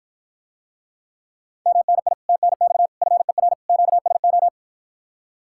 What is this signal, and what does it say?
Morse code, heard as a keyed tone, about 53 wpm, 700 Hz. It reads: MNI TNX FER QSO